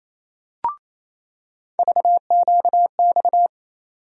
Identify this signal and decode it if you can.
Morse code, heard as a keyed tone, about 28 wpm, 700 Hz. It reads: VQX
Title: USA Call Sign Postfixes - 3 Letters 28wpm